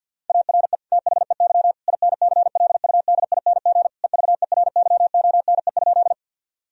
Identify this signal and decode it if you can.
Morse code, heard as a keyed tone, about 50 wpm, 700 Hz. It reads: ARE THEY INCLUDING EVERYONE?
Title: Sentences from Top 1000 Words - Review Part 4 - 50wpm